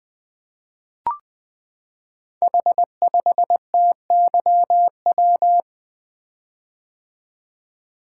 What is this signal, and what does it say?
Morse code, heard as a keyed tone, about 20 wpm, 700 Hz. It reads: H5TYW
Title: Code Groups - 5 Letter:Numbers - 20wpm